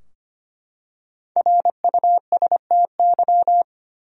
Morse code, heard as a keyed tone, about 25 wpm, 700 Hz. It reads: RUSTY
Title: USA Common Names - Max 5 Letters - 25wpm